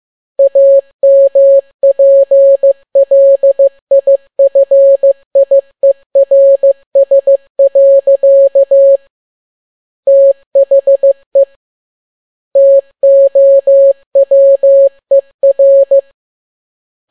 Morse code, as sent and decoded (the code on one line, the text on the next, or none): .- -- .--. .-.. .. ..-. .. .-. ... .-.-.- / - .... / - --- .-- . .-.
AMPLIFIERS. THE TOWER